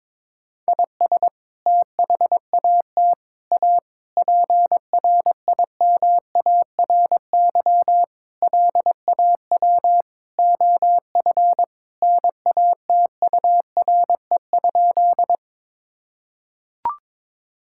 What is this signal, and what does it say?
Morse code, heard as a keyed tone, about 22 wpm, 700 Hz. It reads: IS THAT A PRIMARY LAW OF NATURE?
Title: Sentences from Top 1000 Words - Review Part 5 - 22wpm